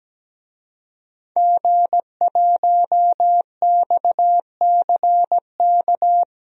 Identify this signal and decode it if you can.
Morse code, heard as a keyed tone, about 17 wpm, 700 Hz. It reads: G1XCK
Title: DXCC-100 Call Signs - 5 Characters - 17wpm